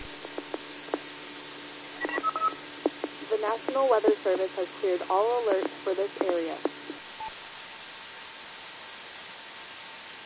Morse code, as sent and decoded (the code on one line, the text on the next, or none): none